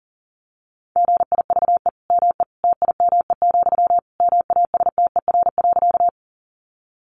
Morse code, heard as a keyed tone, about 40 wpm, 700 Hz. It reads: GIVE ME TIME, MASTER.
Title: Sentences from Top 1300 Words 40wpm